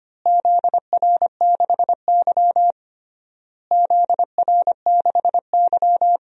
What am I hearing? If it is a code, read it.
Morse code, heard as a keyed tone, about 25 wpm, 700 Hz. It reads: ZR6Y ZR6Y
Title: DXCC-100 Call Signs - 4 Characters - Repeated 3x - 25wpm